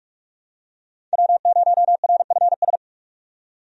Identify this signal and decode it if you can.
Morse code, heard as a keyed tone, about 42 wpm, 700 Hz. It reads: W0RFS
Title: USA Call Signs - 5 Characters - 45wpm